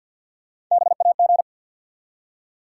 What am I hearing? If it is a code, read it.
Morse code, heard as a keyed tone, about 49 wpm, 700 Hz. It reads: BAG
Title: Three Letter Words - 50wpm